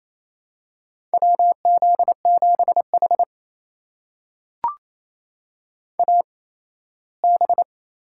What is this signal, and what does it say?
Morse code, heard as a keyed tone, about 28 wpm, spaced as for 17 wpm, 700 Hz. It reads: WZ7H AB
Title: USA Call Signs - 4 Characters - 8x Character Spacing - 28wpm